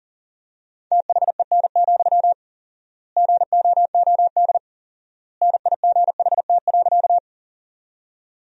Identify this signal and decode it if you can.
Morse code, heard as a keyed tone, about 37 wpm, 700 Hz. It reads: THEN, GOOD NIGHT.
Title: 4x Word-Spacing - Sentences from Top 300 Words - 40wpm